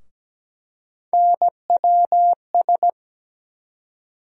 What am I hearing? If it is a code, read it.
Morse code, heard as a keyed tone, about 17 wpm, 700 Hz. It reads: NWS
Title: USA Call Sign Postfixes - 3 Letters 17wpm